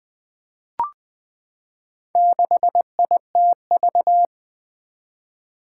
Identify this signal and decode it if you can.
Morse code, heard as a keyed tone, about 20 wpm, 700 Hz. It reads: 6ITV
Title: Four Challenging Letter-Number Combinations 20wpm